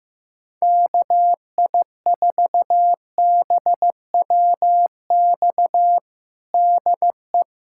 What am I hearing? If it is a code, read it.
Morse code, heard as a keyed tone, about 15 wpm, 700 Hz. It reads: KI4BWX DE
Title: USA Callsign DE Callsign - 6 Characters - 15wpm